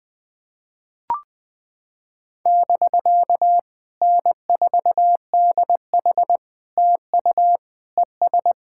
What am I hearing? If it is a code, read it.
Morse code, heard as a keyed tone, about 20 wpm, 700 Hz. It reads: <BK> N4DH TU ES
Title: POTA QSO Practice - 20wpm